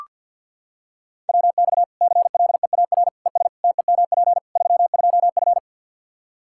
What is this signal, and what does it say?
Morse code, heard as a keyed tone, about 46 wpm, 700 Hz. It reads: WX CLEAR ES TEMP 32F
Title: Practice QSO - Exchange 2 - Info - 50wpm